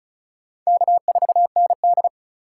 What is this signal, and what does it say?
Morse code, heard as a keyed tone, about 36 wpm, 700 Hz. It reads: K4ND
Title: USA Call Signs - 4 Characters - 4x Character Spacing - 35wpm